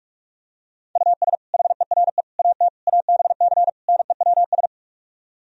Morse code, heard as a keyed tone, about 48 wpm, 700 Hz. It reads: US HERE AT ABC NEWS